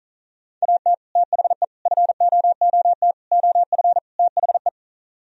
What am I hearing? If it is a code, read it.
Morse code, heard as a keyed tone, about 43 wpm, 700 Hz. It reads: AT THE FOOT OF THE
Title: Speed-Racing - Sets of 5 Words 35wpm